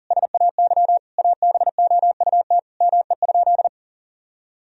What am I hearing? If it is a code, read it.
Morse code, heard as a keyed tone, about 40 wpm, 700 Hz. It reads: SAY ABOUT ME?
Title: Sentences from Top 1000 Words - Review Part 4 - 40wpm